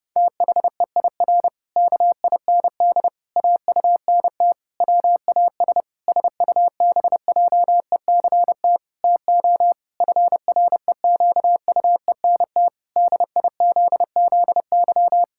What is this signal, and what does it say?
Morse code, heard as a keyed tone, about 32 wpm, 700 Hz. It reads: THEIR KIND AUNT WAS SUBJECT TO FREQUENT DIZZY
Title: Pangram Challenge v2 - 30wpm